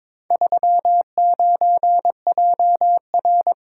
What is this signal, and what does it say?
Morse code, heard as a keyed tone, about 22 wpm, 700 Hz. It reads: 39JR
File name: practice / 22wpm / Four Letter-Number Combinations - 22wpm